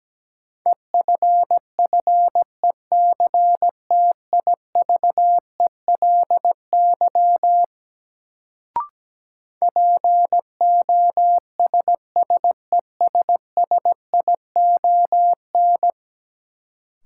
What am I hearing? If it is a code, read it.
Morse code, heard as a keyed tone, about 17 wpm, 700 Hz. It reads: EFFECTIVELY POSSESSION
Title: Sentences from Top 1400 Words - New Word Drill Alt - 17wpm